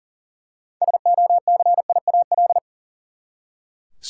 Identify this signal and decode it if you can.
Morse code, heard as a keyed tone, about 40 wpm, 700 Hz. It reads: SOCIAL
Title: K1USN SST QSO Practice 40wpm